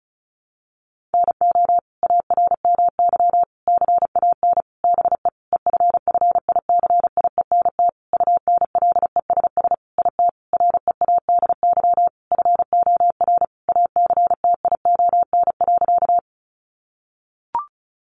Morse code, heard as a keyed tone, about 34 wpm, 700 Hz. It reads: NO ARMY CAN BE EFFICIENT UNLESS IT READY FOR ACTION.
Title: Sentences from Top 1300 Words 35wpm